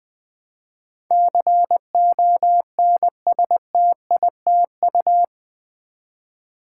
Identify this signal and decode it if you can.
Morse code, heard as a keyed tone, about 20 wpm, 700 Hz. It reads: CONSTITU
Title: Build A Word - 4 Letters At A Time - 12-Letter Words - 20wpm